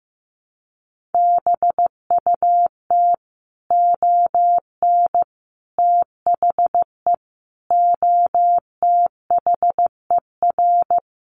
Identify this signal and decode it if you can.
Morse code, heard as a keyed tone, about 15 wpm, 700 Hz. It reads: BUT ON THE OTHER
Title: Sets of 5 Words 15wpm